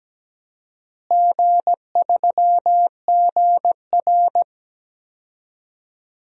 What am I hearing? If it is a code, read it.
Morse code, heard as a keyed tone, about 17 wpm, 700 Hz. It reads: G3GR
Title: DXCC-100 Call Signs - 4 Characters - Repeated 2x - 17wpm